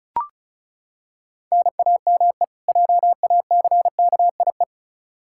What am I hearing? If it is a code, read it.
Morse code, heard as a keyed tone, about 36 wpm, 700 Hz. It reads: NAME JACKIE
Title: Practice QSO - Exchange 1 - Beginning - 35wpm